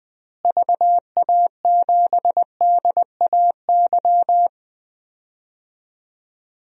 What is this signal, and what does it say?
Morse code, heard as a keyed tone, about 20 wpm, 700 Hz. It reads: VA7DAY
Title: DXCC-100 Call Signs - 6 Characters - Repeated 3x - 20wpm